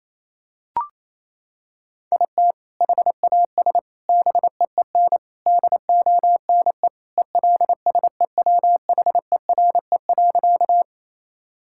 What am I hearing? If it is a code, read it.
Morse code, heard as a keyed tone, about 28 wpm, 700 Hz. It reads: IT HAS BEEN DONE ELSEWHERE.
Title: Sentences from Top 1600 Words 28wpm